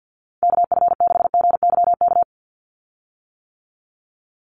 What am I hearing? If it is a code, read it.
Morse code, heard as a keyed tone, about 48 wpm, 700 Hz. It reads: KF6GYK
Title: USA Call Signs 50wpm